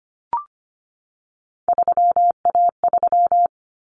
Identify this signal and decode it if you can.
Morse code, heard as a keyed tone, about 25 wpm, 700 Hz. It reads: 3A3
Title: Common Cut Numbers - 3-Digit 25wpm